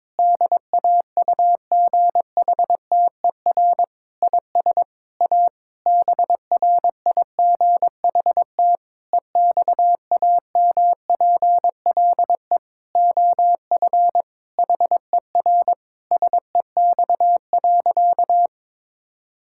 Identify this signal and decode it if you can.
Morse code, heard as a keyed tone, about 22 wpm, 700 Hz. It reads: DAUGHTER IS A BRIGHT EXAMPLE OF HER SEX.
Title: Sentences from Top 1600 Words 22wpm